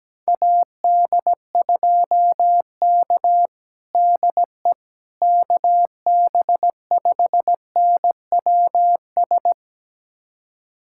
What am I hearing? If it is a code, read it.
Morse code, heard as a keyed tone, about 17 wpm, 700 Hz. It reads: AD2K DE KB5NWS
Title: USA Callsign DE Callsign 17wpm